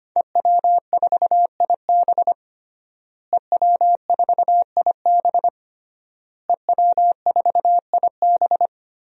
Morse code, heard as a keyed tone, about 25 wpm, 700 Hz. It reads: EW4IB EW4IB EW4IB
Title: DXCC-100 Call Signs - 5 Characters - Repeated 3x - 25wpm